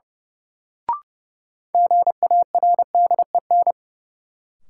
Morse code, heard as a keyed tone, about 30 wpm, 700 Hz. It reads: GARDEN
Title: Sentences from Top 700 Words - New Word Drill - 30wpm